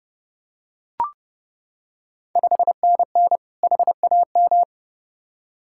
Morse code, heard as a keyed tone, about 30 wpm, 700 Hz. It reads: 5NN HAM
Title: Ontario QSO Party Exchange 30wpm